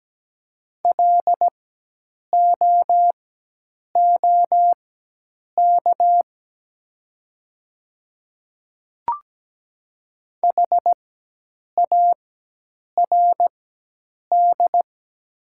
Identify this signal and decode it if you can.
Morse code, heard as a keyed tone, about 17 wpm, spaced as for 8 wpm, 700 Hz. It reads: LOOK HARD
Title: 4x Character Spacing - Top 300 Words - 17wpm